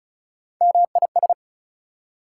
Morse code, heard as a keyed tone, about 47 wpm, spaced as for 41 wpm, 700 Hz. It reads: MIS
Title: Speed-Racing Prefixes and Suffixes 35wpm